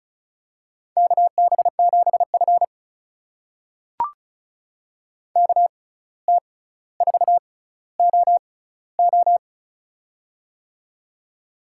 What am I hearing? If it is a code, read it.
Morse code, heard as a keyed tone, about 34 wpm, spaced as for 17 wpm, 700 Hz. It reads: KB7F KT4OO
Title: USA Call Signs - 6x Character Spacing - 35wpm